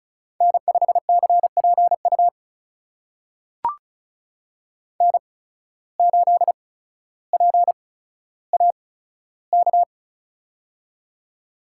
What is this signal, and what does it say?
Morse code, heard as a keyed tone, about 36 wpm, spaced as for 15 wpm, 700 Hz. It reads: N5CPU N8PAK
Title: USA Call Signs - 5 Characters - 8x Character Spacing - 35wpm